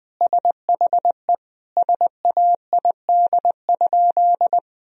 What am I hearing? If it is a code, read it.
Morse code, heard as a keyed tone, about 20 wpm, 700 Hz. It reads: SHE SAID?
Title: Sentences from Top 1000 Words - Review Part 1 - 20wpm